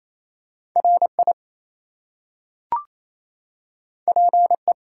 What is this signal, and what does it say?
Morse code, heard as a keyed tone, about 28 wpm, 700 Hz. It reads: RI PE
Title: US States and Canadian Provinces 28wpm